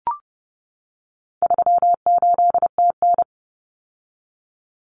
Morse code, heard as a keyed tone, about 30 wpm, 700 Hz. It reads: 38TN
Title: Common Cut Numbers - 4-Digit 30wpm